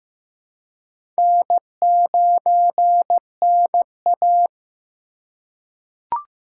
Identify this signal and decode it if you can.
Morse code, heard as a keyed tone, about 15 wpm, 700 Hz. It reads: N9NA
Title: USA Call Signs - 4 Characters - 2x Character Spacing - 15wpm